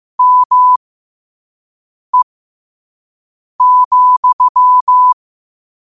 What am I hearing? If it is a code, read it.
Morse code, heard as a keyed tone, about 15 wpm, spaced as for 5 wpm, 1000 Hz. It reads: ME,
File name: qotdfiles / 5wpm